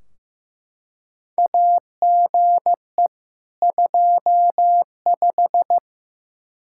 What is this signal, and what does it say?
Morse code, heard as a keyed tone, about 15 wpm, 700 Hz. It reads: AGE 25
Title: Practice QSO - Exchange 3 - More Info - 15wpm